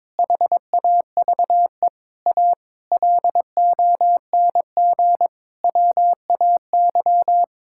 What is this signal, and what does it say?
Morse code, heard as a keyed tone, about 21 wpm, 700 Hz. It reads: HAVE A LONG WAY
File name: Sets of 9 Words - 22wpm